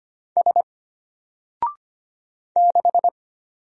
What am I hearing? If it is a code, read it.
Morse code, heard as a keyed tone, about 25 wpm, 700 Hz. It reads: S 6
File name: Single-Letter-Number-25wpm